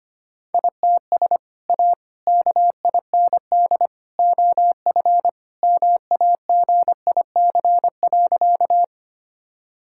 Morse code, heard as a keyed tone, about 25 wpm, 700 Hz. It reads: ITS A KIND OF MAGIC.
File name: practice / 25wpm / Sentences with 3-Word Phrases - 25wpm